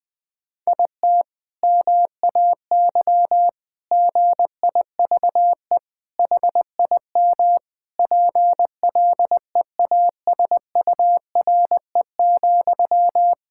Morse code, heard as a keyed tone, about 20 wpm, 700 Hz. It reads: IT MAY GIVE HIM PLEASURE,
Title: Sentences from Top 1600 Words 20wpm